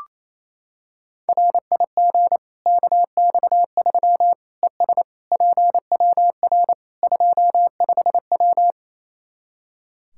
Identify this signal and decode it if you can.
Morse code, heard as a keyed tone, about 28 wpm, 700 Hz. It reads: RIG KX3 ES PWR 25W
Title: Practice QSO - Exchange 2 - Info - 28wpm